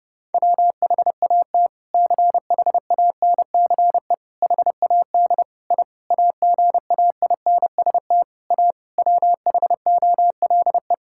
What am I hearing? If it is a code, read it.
Morse code, heard as a keyed tone, about 32 wpm, 700 Hz. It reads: WHAT CHANCE HAD I AGAINST A WHOLE
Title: Sentences from Top 1000 Words - Review Part 6 - 30wpm